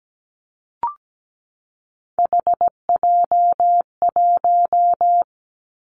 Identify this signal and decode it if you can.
Morse code, heard as a keyed tone, about 17 wpm, 700 Hz. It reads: HJ1
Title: Three Challenging Letter-Number Combinations 17wpm